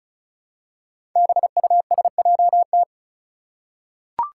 Morse code, heard as a keyed tone, about 34 wpm, 700 Hz. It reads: BUSJT
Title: Five Challenging Letter-Number Combinations 35wpm